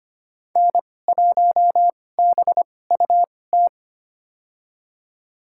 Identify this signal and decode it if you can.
Morse code, heard as a keyed tone, about 25 wpm, spaced as for 18 wpm, 700 Hz. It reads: N1BUT
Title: USA Call Signs - 5 Characters - 2x Character Spacing - 25wpm